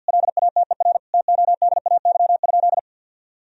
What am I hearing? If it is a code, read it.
Morse code, heard as a keyed tone, about 52 wpm, 700 Hz. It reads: LATER TODAY?
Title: Sentences from Top 1000 Words - Review Part 2 - 50wpm